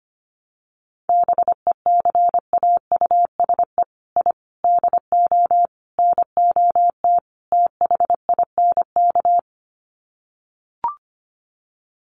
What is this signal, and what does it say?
Morse code, heard as a keyed tone, about 25 wpm, 700 Hz. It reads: BECAUSE I DO NOT THINK
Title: Sets of 5 Words Encore 25wpm